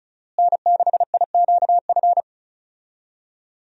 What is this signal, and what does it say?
Morse code, heard as a keyed tone, about 36 wpm, 700 Hz. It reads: N6IQF
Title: USA Call Signs - 5 Characters - Repeated 3x - 35wpm